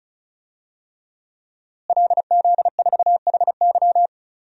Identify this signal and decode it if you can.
Morse code, heard as a keyed tone, about 36 wpm, 700 Hz. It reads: LZ4HY
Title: DXCC-100 Call Signs - 5 Characters - Repeated 3x - 35wpm